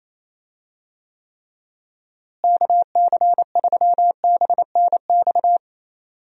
Morse code, heard as a keyed tone, about 28 wpm, 700 Hz. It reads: KC3BNX
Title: USA Call Signs - 6 Characters - Repeated 2x - 28wpm